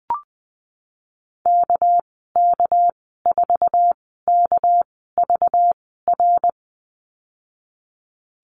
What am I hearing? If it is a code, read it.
Morse code, heard as a keyed tone, about 19 wpm, spaced as for 14 wpm, 700 Hz. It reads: KK4KVR
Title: 238 Lesson 38g - Letter q - Call Signs 20wpm